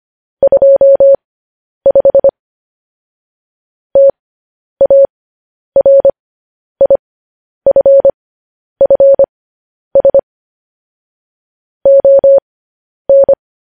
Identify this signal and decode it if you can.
Morse code, heard as a keyed tone, about 25 wpm, spaced as for 10 wpm, 550 Hz. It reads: 25 TARIFFS ON
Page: news-headlines-in-morse-code-at-10-wpm